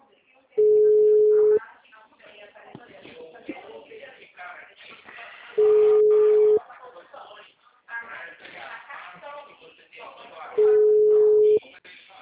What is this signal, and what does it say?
Ringback tone (Europe, 425 Hz), with speech.